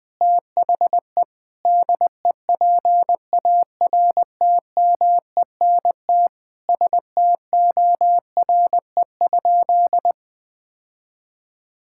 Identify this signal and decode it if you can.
Morse code, heard as a keyed tone, about 20 wpm, 700 Hz. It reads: THE DEPARTMENT STORE?
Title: Sentences from Top 1000 Words - Review Part 5 - 20wpm